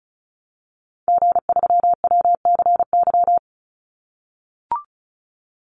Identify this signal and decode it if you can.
Morse code, heard as a keyed tone, about 34 wpm, 700 Hz. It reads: G3WCY
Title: Call Signs - 35wpm